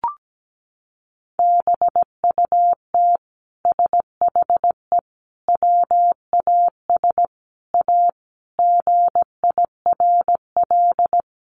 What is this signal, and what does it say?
Morse code, heard as a keyed tone, about 17 wpm, 700 Hz. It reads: BUT SHE WAS A GIRL
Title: Sentences from Top 1400 Words 17wpm